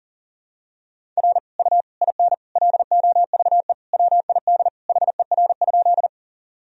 Morse code, heard as a keyed tone, about 40 wpm, 700 Hz. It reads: R U IN LOVE WID HER?
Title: Sentences from Top 400 Words and Abbreviations 40wpm